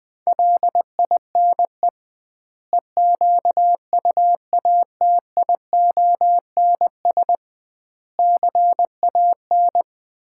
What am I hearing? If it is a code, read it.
Morse code, heard as a keyed tone, about 21 wpm, 700 Hz. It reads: LINE EQUATIONS CAN
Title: 2x Word-Spacing - Sets of 5 Words - 20wpm